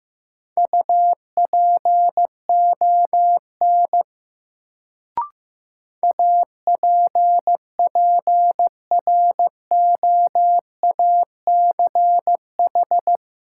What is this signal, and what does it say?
Morse code, heard as a keyed tone, about 15 wpm, 700 Hz. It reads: UPON APPROACH
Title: Sentences from Top 1000 Words - Review Part 6 - New Word Drill - 15wpm